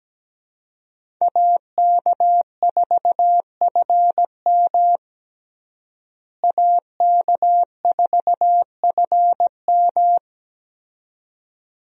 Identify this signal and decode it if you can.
Morse code, heard as a keyed tone, about 17 wpm, 700 Hz. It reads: AK4FM AK4FM
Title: USA Call Signs - 5 Characters - Repeated 2x - 17wpm